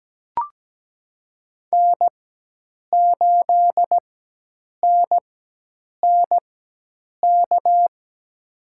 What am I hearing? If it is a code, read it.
Morse code, heard as a keyed tone, about 17 wpm, spaced as for 8 wpm, 700 Hz. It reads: N8NNK
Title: USA Call Signs - 5 Characters - 4x Character Spacing - 17wpm